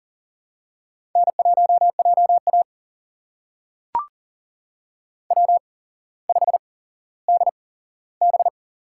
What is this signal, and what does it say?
Morse code, heard as a keyed tone, about 40 wpm, spaced as for 17 wpm, 700 Hz. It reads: N1JA W5DB
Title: USA Call Signs - 4 Characters - 8x Character Spacing - 40wpm